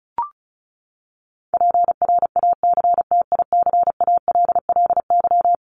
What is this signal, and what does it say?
Morse code, heard as a keyed tone, about 34 wpm, 700 Hz. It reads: PRACTICALLY
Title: LY Suffix - Top 100 Words with Base Words 35wpm